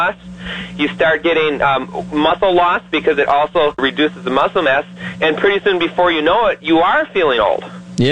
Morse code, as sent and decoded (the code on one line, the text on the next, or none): none